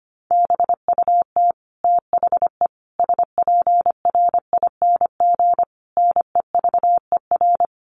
Morse code, heard as a keyed tone, about 25 wpm, 700 Hz. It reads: BUT THE SPRING NEVER
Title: Sentences from Top 1500 Words 25wpm